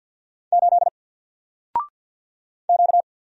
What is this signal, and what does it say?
Morse code, heard as a keyed tone, about 53 wpm, 700 Hz. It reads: 8 Q
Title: Single Letter-Number Alt 50wpm